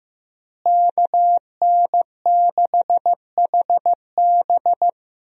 Morse code, heard as a keyed tone, about 15 wpm, 700 Hz. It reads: KN6HB
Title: USA Call Signs - 5 Characters - Repeated 3x - 15wpm